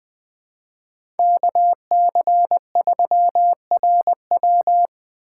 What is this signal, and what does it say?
Morse code, heard as a keyed tone, about 21 wpm, 700 Hz. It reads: KC3RW
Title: 2023 POTA Activator Call Signs - 20wpm